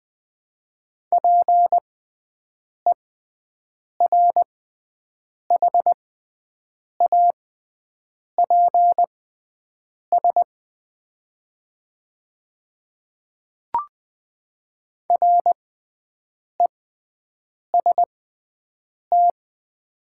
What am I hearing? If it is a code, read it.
Morse code, heard as a keyed tone, about 20 wpm, spaced as for 7 wpm, 700 Hz. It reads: PERHAPS REST
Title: 6x Character Spacing - Top 500 Words - 20wpm